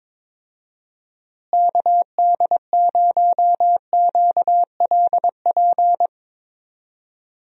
Morse code, heard as a keyed tone, about 22 wpm, 700 Hz. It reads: KD0QLP
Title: USA Call Signs - 6 Characters - Repeated 3x - 22wpm